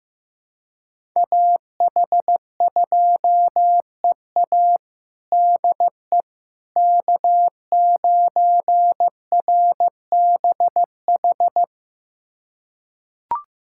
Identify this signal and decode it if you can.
Morse code, heard as a keyed tone, about 15 wpm, 700 Hz. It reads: AH2EA DE K9RBH
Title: USA Callsign DE Callsign - 5 Characters - 15wpm